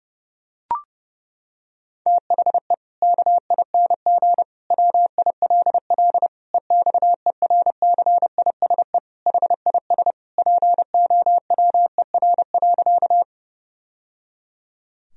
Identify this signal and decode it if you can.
Morse code, heard as a keyed tone, about 30 wpm, 700 Hz. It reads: THE KING WILL EXERCISE HIS POWER.
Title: Sentences from Top 1000 Words - Review Part 4 - 30wpm